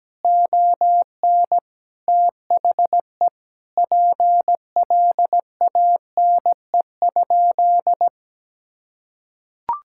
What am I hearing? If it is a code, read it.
Morse code, heard as a keyed tone, about 17 wpm, 700 Hz. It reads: ON THE PLANE?
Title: Sentences from Top 1000 Words - Review Part 6 - 17wpm